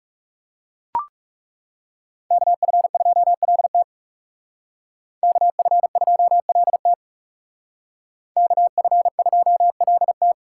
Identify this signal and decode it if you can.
Morse code, heard as a keyed tone, about 38 wpm, 700 Hz. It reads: KF2LT KF2LT KF2LT